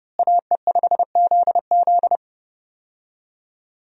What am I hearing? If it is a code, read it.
Morse code, heard as a keyed tone, about 30 wpm, 700 Hz. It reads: AE5ZZ
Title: USA Call Signs - 5 Characters - Repeated 2x - 30wpm